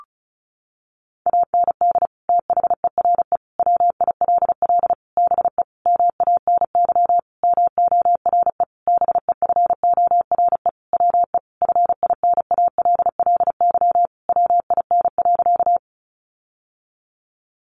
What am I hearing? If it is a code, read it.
Morse code, heard as a keyed tone, about 34 wpm, 700 Hz. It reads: AND THERE WILL BE MANY MORE BEFORE WE FINALLY WIN.
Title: Sentences from Top 1500 Words 35wpm